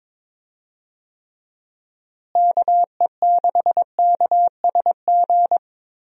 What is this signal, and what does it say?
Morse code, heard as a keyed tone, about 22 wpm, 700 Hz. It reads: KE6KSG
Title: USA Call Signs - 6 Characters - Repeated 2x - 22wpm